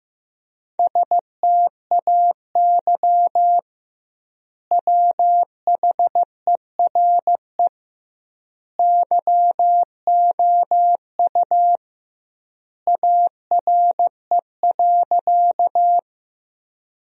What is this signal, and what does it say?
Morse code, heard as a keyed tone, about 15 wpm, 700 Hz. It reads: STAY WHERE YOU ARE.
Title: 2x Word-Spacing - Sentences from Top 500 Words - 15wpm